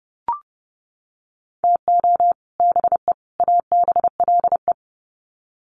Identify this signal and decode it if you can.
Morse code, heard as a keyed tone, about 30 wpm, 700 Hz. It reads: TO BE ABLE
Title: Sets of 3 Words 30wpm